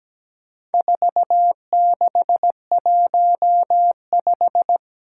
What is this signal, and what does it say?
Morse code, heard as a keyed tone, about 17 wpm, 700 Hz. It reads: 4615
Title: Four Challenging Letter-Number Combinations 17wpm